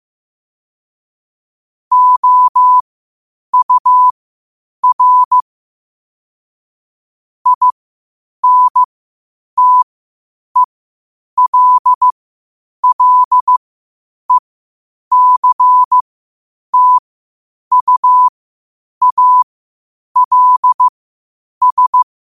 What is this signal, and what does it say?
Morse code, heard as a keyed tone, about 15 wpm, spaced as for 8 wpm, 1000 Hz. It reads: OUR INTELLECTUALS
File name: qotdfiles / 8wpm